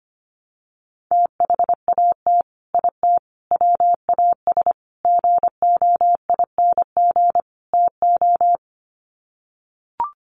Morse code, heard as a keyed tone, about 25 wpm, 700 Hz. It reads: THAT IT WAS GOING TO
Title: Sets of 5 Words Encore2 25wpm